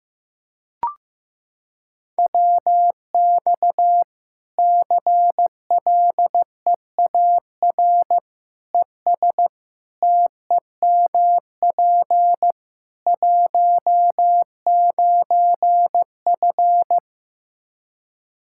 Morse code, heard as a keyed tone, about 15 wpm, 700 Hz. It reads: WX CLEAR ES TEMP 19F
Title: Practice QSO - Exchange 2 - Info - 15wpm